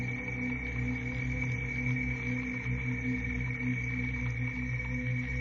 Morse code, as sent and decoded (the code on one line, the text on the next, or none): none